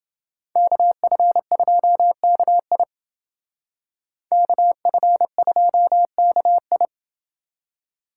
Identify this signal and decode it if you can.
Morse code, heard as a keyed tone, about 29 wpm, 700 Hz. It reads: KF2KI KF2KI